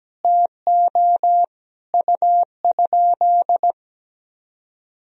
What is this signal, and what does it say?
Morse code, heard as a keyed tone, about 17 wpm, 700 Hz. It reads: TO U?
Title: Sentences from Top 100 Words and Abbreviations 17wpm